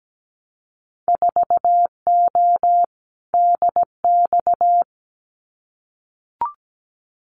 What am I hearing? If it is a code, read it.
Morse code, heard as a keyed tone, about 17 wpm, 700 Hz. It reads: 4O DX
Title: Winter Field Day Exchanges Encore 17wpm